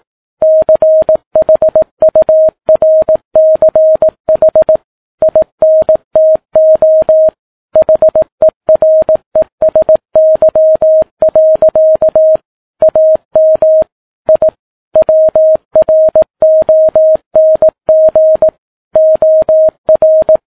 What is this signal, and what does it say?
Morse code, heard as a keyed tone, about 18 wpm, 630 Hz. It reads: CHURCH INTO HERESY. AM I WRONG OR
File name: Ham Radio Sat